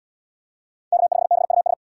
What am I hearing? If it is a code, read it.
Morse code, heard as a keyed tone, about 67 wpm, 700 Hz. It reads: K5BUS